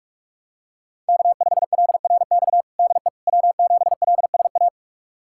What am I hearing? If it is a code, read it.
Morse code, heard as a keyed tone, about 47 wpm, 700 Hz. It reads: K5LRX DE W7LSA
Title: USA Callsign DE Callsign 45wpm